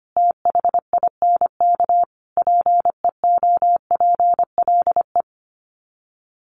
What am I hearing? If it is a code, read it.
Morse code, heard as a keyed tone, about 25 wpm, 700 Hz. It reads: THINK PEOPLE
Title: Sets of 5 Words Encore2 25wpm